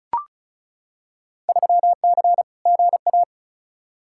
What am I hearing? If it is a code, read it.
Morse code, heard as a keyed tone, about 36 wpm, 700 Hz. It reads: 3C GA
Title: Field Day Exchanges Encore - 35wpm